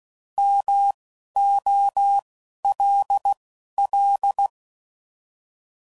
Morse code, heard as a keyed tone, about 16 wpm, spaced as for 11 wpm, 800 Hz. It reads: MOLL